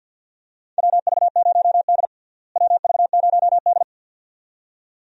Morse code, heard as a keyed tone, about 48 wpm, 700 Hz. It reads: WV0D WV0D
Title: USA Call Signs - 4 Characters - Repeated 2x - 50wpm